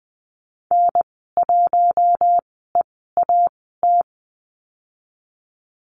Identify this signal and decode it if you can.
Morse code, heard as a keyed tone, about 20 wpm, spaced as for 15 wpm, 700 Hz. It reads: N1EAT
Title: 048 Lesson 9g - Number 4 - Call Signs 20wpm